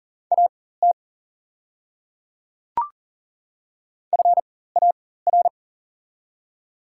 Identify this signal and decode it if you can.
Morse code, heard as a keyed tone, about 40 wpm, spaced as for 19 wpm, 690 Hz. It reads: AT FAR
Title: 4x Character Spacing - Top 500 Words - 40wpm